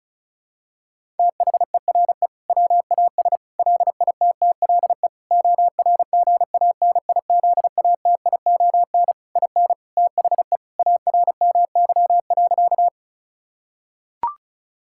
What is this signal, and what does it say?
Morse code, heard as a keyed tone, about 36 wpm, 700 Hz. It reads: THERE WAS LITTLE ORGANIZATION IN THE ARMY.
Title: Sentences from Top 1000 Words - Review Part 5 - 35wpm